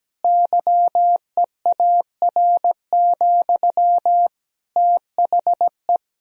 Morse code, heard as a keyed tone, about 17 wpm, 700 Hz. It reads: YEAR, THE
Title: Sentences from Top 1000 Words - Review Part 2 - 17wpm